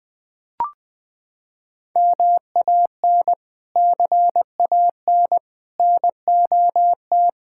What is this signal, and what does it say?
Morse code, heard as a keyed tone, about 20 wpm, 700 Hz. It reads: MAN CAN NOT
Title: Sentences from Top 1000 Words - Review Part 5 - 20wpm